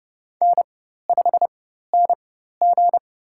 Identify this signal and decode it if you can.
Morse code, heard as a keyed tone, about 30 wpm, spaced as for 14 wpm, 700 Hz. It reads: N5NG
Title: USA Call Signs - 4x Character Spacing - 30wpm